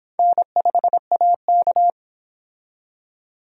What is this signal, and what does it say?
Morse code, heard as a keyed tone, about 26 wpm, 700 Hz. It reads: N5AK